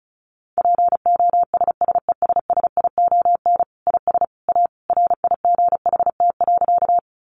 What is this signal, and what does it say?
Morse code, heard as a keyed tone, about 34 wpm, 700 Hz. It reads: POSSESSION IS A RIGHT.
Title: Sentences from Top 1400 Words 35wpm